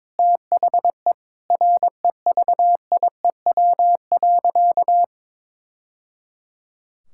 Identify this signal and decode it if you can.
Morse code, heard as a keyed tone, about 22 wpm, 700 Hz. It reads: THE REVIEW.
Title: Sentences from Top 1000 Words - Review Part 6 - 22wpm